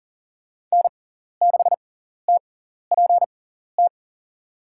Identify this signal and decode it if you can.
Morse code, heard as a keyed tone, about 40 wpm, spaced as for 14 wpm, 690 Hz. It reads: N6TPT
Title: USA Call Signs - 5 Characters - 6x Character Spacing - 40wpm